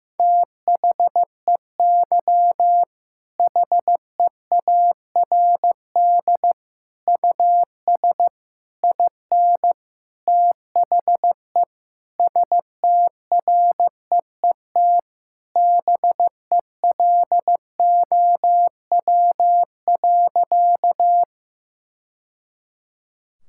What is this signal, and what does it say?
Morse code, heard as a keyed tone, about 15 wpm, 700 Hz. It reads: THEY HEARD US IN THE STREET BELOW.
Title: Sentences from Top 1000 Words - Review Part 5 - 15wpm